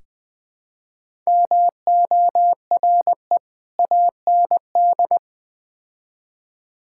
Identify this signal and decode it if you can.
Morse code, heard as a keyed tone, about 20 wpm, 700 Hz. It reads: MORE AND
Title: Build A Phrase - 3-Word Phrases - 20wpm